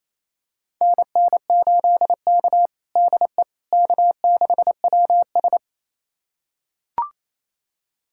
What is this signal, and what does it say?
Morse code, heard as a keyed tone, about 28 wpm, 700 Hz. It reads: NN8K DE K6WS
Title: USA Callsign DE Callsign - 4 Characters - 28wpm